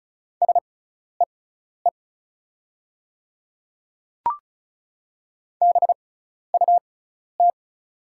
Morse code, heard as a keyed tone, about 34 wpm, spaced as for 12 wpm, 700 Hz. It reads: SEE BUT
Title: 6x Character Spacing - Top 100 Words - 35wpm